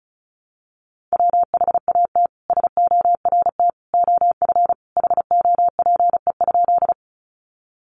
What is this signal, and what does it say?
Morse code, heard as a keyed tone, about 34 wpm, 700 Hz. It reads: WHAT SORT OF HOPE?
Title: Sentences from Top 1300 Words 35wpm